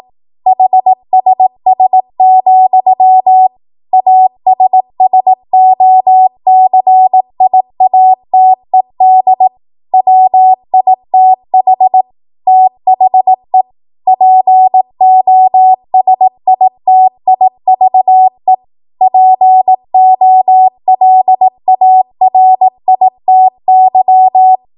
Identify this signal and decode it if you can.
Morse code, heard as a keyed tone, about 18 wpm, 750 Hz. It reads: HSS, ASSOCIATED WITH THE POSITIVE POLARITY
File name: Morse / 18WPM